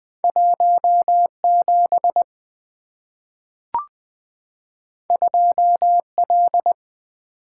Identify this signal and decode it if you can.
Morse code, heard as a keyed tone, about 20 wpm, 700 Hz. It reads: 17 2L
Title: Two Letter-Number Combinations - 20wpm